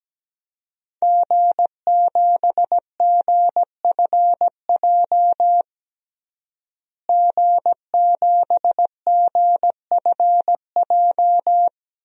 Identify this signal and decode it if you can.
Morse code, heard as a keyed tone, about 17 wpm, 700 Hz. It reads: G7GFJ G7GFJ
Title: DXCC-100 Call Signs - 5 Characters - Repeated 3x - 17wpm